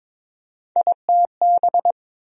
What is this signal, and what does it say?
Morse code, heard as a keyed tone, about 22 wpm, 700 Hz. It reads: ITB
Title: USA Call Sign Postfixes - 3 Letters 22wpm